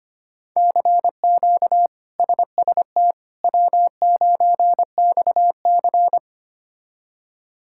Morse code, heard as a keyed tone, about 25 wpm, 700 Hz. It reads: CQ SST W9XC
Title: K1USN SST QSO Practice - 25wpm